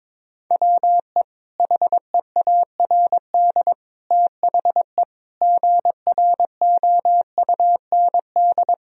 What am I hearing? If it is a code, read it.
Morse code, heard as a keyed tone, about 21 wpm, 700 Hz. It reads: WE HEARD THE GROUND
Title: Sentences from Top 1000 Words - Review Part 2 - 22wpm